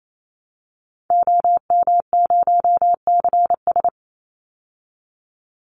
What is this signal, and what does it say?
Morse code, heard as a keyed tone, about 28 wpm, 700 Hz. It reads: OM0CS
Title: Call Signs Repeated 2x - 28wpm